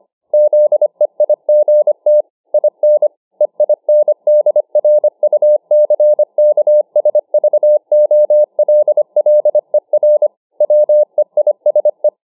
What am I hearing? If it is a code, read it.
Morse code, heard as a keyed tone, about 25 wpm, 600 Hz. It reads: ZEIGT IN EINDRUCKSVOLLER WEISE